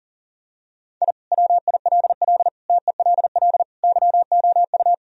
Morse code, heard as a keyed tone, about 40 wpm, 700 Hz. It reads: I WILL TELL YOU
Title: Sentences from Top 1000 Words - Review Part 1 - 40wpm